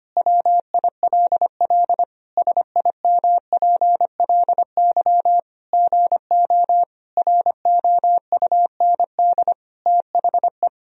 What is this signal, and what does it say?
Morse code, heard as a keyed tone, about 25 wpm, 700 Hz. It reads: WILL SIMPLY GO ROUND THE
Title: Sentences from Top 1000 Words - Review Part 4 - 25wpm